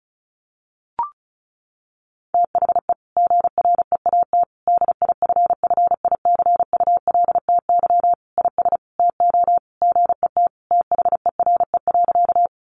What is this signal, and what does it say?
Morse code, heard as a keyed tone, about 34 wpm, 700 Hz. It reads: THE GREAT DIFFICULTY IS TO GET THERE.
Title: Sentences from Top 1500 Words 35wpm